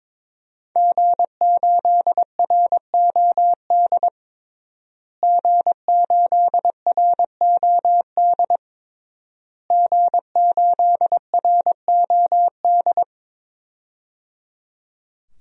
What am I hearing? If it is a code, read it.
Morse code, heard as a keyed tone, about 22 wpm, 700 Hz. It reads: G8ROD G8ROD G8ROD
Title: DXCC-100 Call Signs - 5 Characters - Repeated 3x - 22wpm